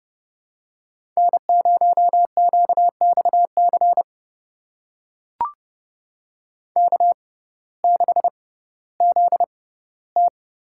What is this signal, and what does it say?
Morse code, heard as a keyed tone, about 30 wpm, spaced as for 17 wpm, 700 Hz. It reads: N0QXC K6ZT
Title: USA Call Signs - 5 Characters - 6x Character Spacing - 30wpm